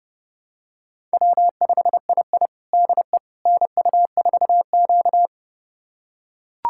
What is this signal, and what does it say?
Morse code, heard as a keyed tone, about 30 wpm, 700 Hz. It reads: W5II DE NU4Q
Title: USA Callsign DE Callsign - 4 Characters - 30wpm